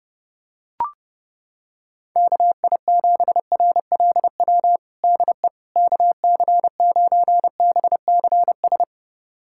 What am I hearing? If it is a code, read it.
Morse code, heard as a keyed tone, about 30 wpm, 700 Hz. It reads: KI7RLW DE KC9BCS
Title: USA Callsign DE Callsign - 6 Characters - 30wpm